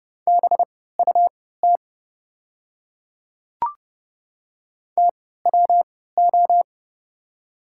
Morse code, heard as a keyed tone, about 30 wpm, spaced as for 17 wpm, 690 Hz. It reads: BUT TWO
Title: 3x Character Spacing - Top 200 Words - 30wpm